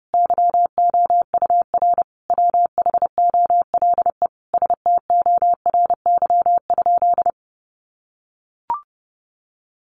Morse code, heard as a keyed tone, about 28 wpm, 700 Hz. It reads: YOUR WHOLE STORY?
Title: Sentences from Top 400 Words 30wpm